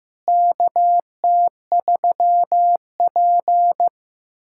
Morse code, heard as a keyed tone, about 15 wpm, 700 Hz. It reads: KT3P
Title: USA Callsign DE Callsign - 4 Characters - 15wpm